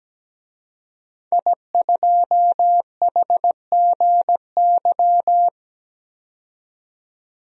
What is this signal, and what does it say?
Morse code, heard as a keyed tone, about 17 wpm, 700 Hz. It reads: I2HGY
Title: DXCC-100 Call Signs - 5 Characters - 17wpm